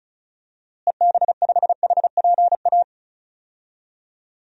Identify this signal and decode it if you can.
Morse code, heard as a keyed tone, about 36 wpm, 700 Hz. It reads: EB5HPA
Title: DXCC-100 Call Signs - 6 Characters - Repeated 3x - 35wpm